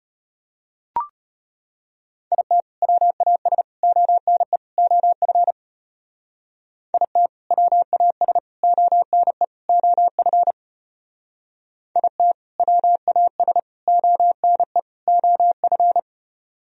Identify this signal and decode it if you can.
Morse code, heard as a keyed tone, about 31 wpm, 700 Hz. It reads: IT WAS ONE OF IT WAS ONE OF IT WAS ONE OF
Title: Speed-Racing - Sets of 4 Words 25wpm